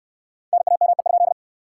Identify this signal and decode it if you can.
Morse code, heard as a keyed tone, about 64 wpm, 700 Hz. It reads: NAME?